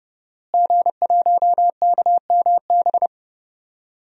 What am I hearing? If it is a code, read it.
Morse code, heard as a keyed tone, about 30 wpm, 700 Hz. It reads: G1KMB
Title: DXCC-100 Call Signs - 5 Characters - Repeated 2x - 30wpm